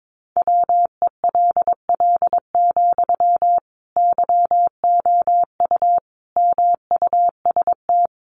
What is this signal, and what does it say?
Morse code, heard as a keyed tone, about 22 wpm, 700 Hz. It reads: WELL, YOU MUST
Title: Sentences from Top 700 Words 22wpm